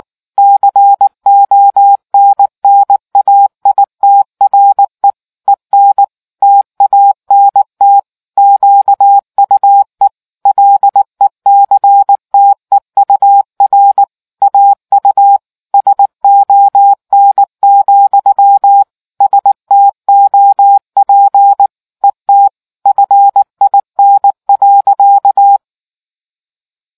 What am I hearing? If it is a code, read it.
Morse code, heard as a keyed tone, about 19 wpm, 800 Hz. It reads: CONNAITRE EN TANT QUE LECTEUR AU SON, STOP ET FIN.